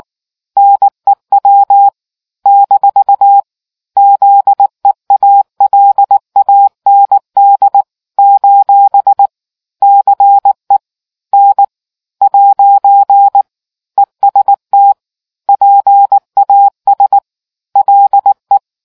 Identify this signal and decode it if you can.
Morse code, heard as a keyed tone, about 19 wpm, 800 Hz. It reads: NEW - ZEALAND : CE N ' EST PAS LE